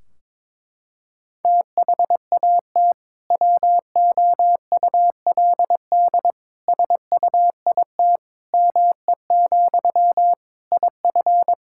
Morse code, heard as a keyed tone, about 22 wpm, 700 Hz. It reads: THAT WOULD SUIT ME, IF
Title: Sentences from Top 1600 Words 22wpm